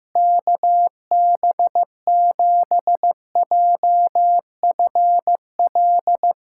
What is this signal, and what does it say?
Morse code, heard as a keyed tone, about 16 wpm, 700 Hz. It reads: KB7JFL
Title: USA Callsign DE Callsign 15wpm